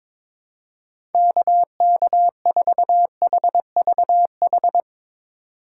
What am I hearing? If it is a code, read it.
Morse code, heard as a keyed tone, about 22 wpm, 700 Hz. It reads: KK4HVH
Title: USA Call Signs - 4x Character Spacing - 22wpm